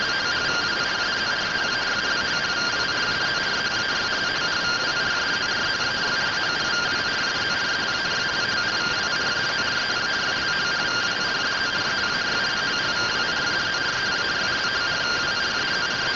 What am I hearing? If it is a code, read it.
Two-tone teleprinter signal, about 170 Hz shift.